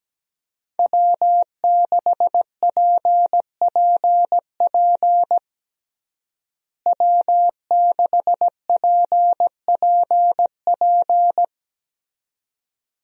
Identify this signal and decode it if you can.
Morse code, heard as a keyed tone, about 17 wpm, 700 Hz. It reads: W6PPP W6PPP
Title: USA Call Signs - 5 Characters - Repeated 2x - 17wpm